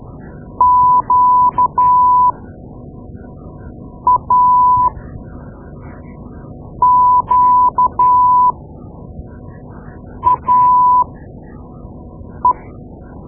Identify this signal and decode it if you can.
Morse code, heard as a keyed tone, about 12 wpm, spaced as for 4 wpm, 1000 Hz. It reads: QAQAE
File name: Morse Lessons / Morse Lessons / Tape-2A